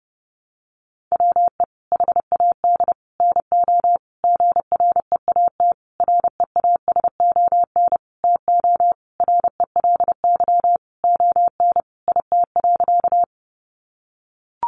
Morse code, heard as a keyed tone, about 30 wpm, 700 Hz. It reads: WE HAD NO GREAT REASON TO RELY ON IT.
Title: Sentences from Top 1200 Words 30wpm